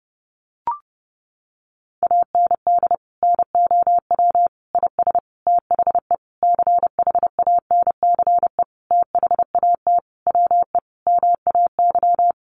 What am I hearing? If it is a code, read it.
Morse code, heard as a keyed tone, about 30 wpm, 700 Hz. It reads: AND NOW IS THE CHANCE THAT WE MAY
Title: Sentences from Top 800 Words 30wpm